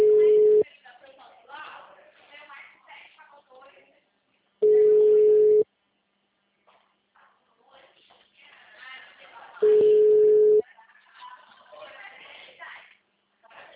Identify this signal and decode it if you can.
Ringback tone (Europe, 425 Hz), with speech.